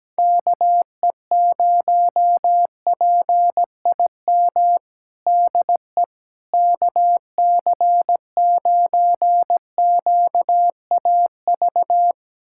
Morse code, heard as a keyed tone, about 17 wpm, 700 Hz. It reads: KE0PIM DE KC9QAV
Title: USA Callsign DE Callsign - 6 Characters - 17wpm